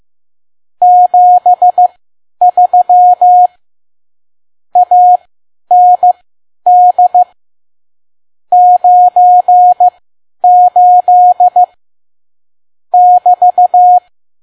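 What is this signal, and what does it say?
Morse code, heard as a keyed tone, about 15 wpm, spaced as for 10 wpm, 720 Hz. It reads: 73 AND 98 <BT>
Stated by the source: Morse code practice at 10 WPM.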